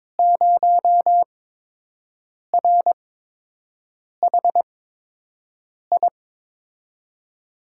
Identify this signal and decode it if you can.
Morse code, heard as a keyed tone, about 21 wpm, spaced as for 6 wpm, 700 Hz. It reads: 0RHI
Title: USA Call Signs - 5 Characters - 8x Character Spacing - 22wpm